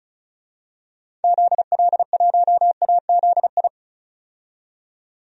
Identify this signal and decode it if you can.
Morse code, heard as a keyed tone, about 34 wpm, 700 Hz. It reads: ZL1AZI
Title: DXCC-100 Call Signs - 6 Characters - 35wpm